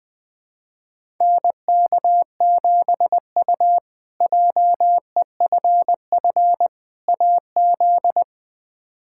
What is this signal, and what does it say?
Morse code, heard as a keyed tone, about 20 wpm, 700 Hz. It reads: NK7U JEFF AZ
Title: K1USN SST QSO Practice - 20wpm